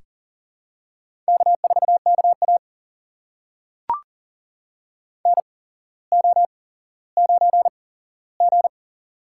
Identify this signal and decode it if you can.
Morse code, heard as a keyed tone, about 40 wpm, spaced as for 17 wpm, 700 Hz. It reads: K4KA NO9G
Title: USA Call Signs - 4 Characters - 8x Character Spacing - 40wpm